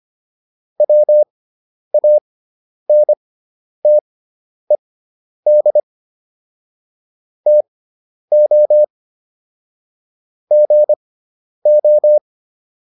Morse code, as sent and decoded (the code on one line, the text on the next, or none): .-- .- -. - . -.. / - --- / --. ---
WANTED TO GO